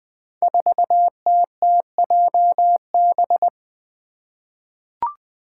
Morse code, heard as a keyed tone, about 20 wpm, 700 Hz. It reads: 4TTJB
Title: Five Challenging Letter-Number Combinations 20wpm